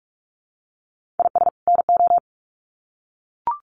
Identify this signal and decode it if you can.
Morse code, heard as a keyed tone, about 44 wpm, 700 Hz. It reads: IS NO
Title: Sets of 2 Words 45wpm